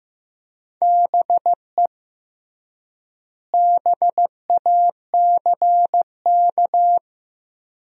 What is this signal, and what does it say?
Morse code, heard as a keyed tone, about 15 wpm, 700 Hz. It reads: BE BACK
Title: 3x Word-Spacing - Sentences from Top 500 Words - 15wpm